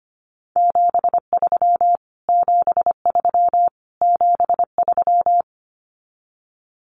Morse code, heard as a keyed tone, about 25 wpm, 700 Hz. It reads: 73 73 73
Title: Abbreviations Repeated 3x - 25wpm